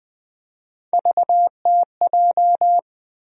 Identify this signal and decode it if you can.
Morse code, heard as a keyed tone, about 20 wpm, 700 Hz. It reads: VTJ
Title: USA Call Sign Postfixes - 3 Letters 20wpm